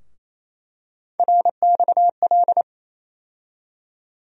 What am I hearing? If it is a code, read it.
Morse code, heard as a keyed tone, about 28 wpm, 700 Hz. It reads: RXL
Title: USA Call Sign Postfixes - 3 Letters 28wpm